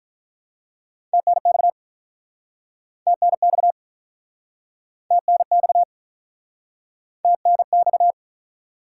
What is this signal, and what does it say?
Morse code, heard as a keyed tone, about 49 wpm, 700 Hz. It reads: TNX TNX TNX TNX